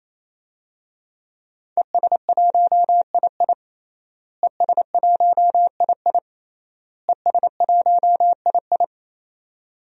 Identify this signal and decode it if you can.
Morse code, heard as a keyed tone, about 28 wpm, 700 Hz. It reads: ES1II ES1II ES1II
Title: DXCC-100 Call Signs - 5 Characters - Repeated 3x - 28wpm